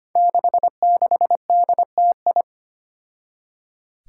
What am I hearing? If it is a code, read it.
Morse code, heard as a keyed tone, about 25 wpm, 700 Hz. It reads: 66DTI
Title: Five Challenging Letter-Number Combinations 25wpm